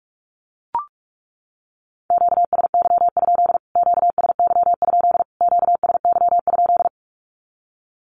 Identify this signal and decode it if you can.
Morse code, heard as a keyed tone, about 44 wpm, 700 Hz. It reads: QSY? QSY? QSY?
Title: Q-Codes Repeated 3x 45wpm